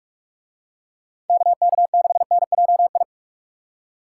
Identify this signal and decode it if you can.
Morse code, heard as a keyed tone, about 44 wpm, 700 Hz. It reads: KK6NJI